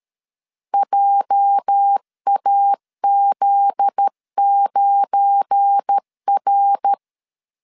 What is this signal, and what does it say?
Morse code, heard as a keyed tone, about 13 wpm, 800 Hz. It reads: JAZ9R
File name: LEZIONI / 8 YJ45 / 8 YJ 12wpm 2235